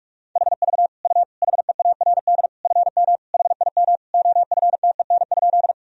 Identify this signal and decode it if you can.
Morse code, heard as a keyed tone, about 42 wpm, 700 Hz. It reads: HV U HEARD FM HIM OFTEN?
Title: Sentences from Top 400 Words and Abbreviations 45wpm